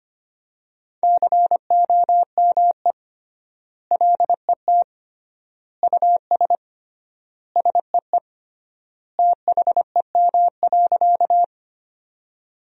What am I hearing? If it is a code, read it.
Morse code, heard as a keyed tone, about 25 wpm, 700 Hz. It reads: COME LET US SEE THEM.
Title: 3x Word-Spacing - Sentences from Top 300 Words - 25wpm